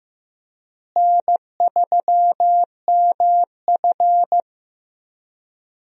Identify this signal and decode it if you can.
Morse code, heard as a keyed tone, about 15 wpm, 700 Hz. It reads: N3MF
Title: USA Call Signs - 4 Characters - Repeated 3x - 15wpm